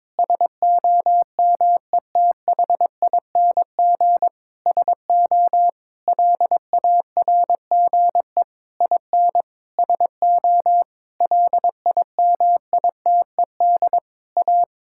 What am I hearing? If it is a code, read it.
Morse code, heard as a keyed tone, about 22 wpm, 700 Hz. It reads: SOMETHING SO LARGE IN SO LIMITED A
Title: Sentences from Top 1000 Words - Review Part 6 - 22wpm